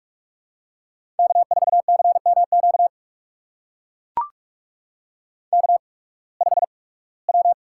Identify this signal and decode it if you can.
Morse code, heard as a keyed tone, about 44 wpm, spaced as for 23 wpm, 700 Hz. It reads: K4CMQ K5W
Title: USA Call Signs - 5 Characters - 8x Character Spacing - 45wpm